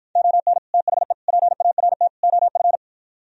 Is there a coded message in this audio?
Yes: Morse code from 0.2 to 2.8 s, heard as a keyed tone, about 53 wpm, 700 Hz. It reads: ON THE PART OF